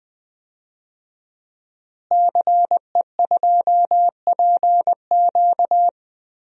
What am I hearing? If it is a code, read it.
Morse code, heard as a keyed tone, about 20 wpm, 700 Hz. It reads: CE2PQ
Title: DXCC-100 Call Signs - 5 Characters - Repeated 2x - 20wpm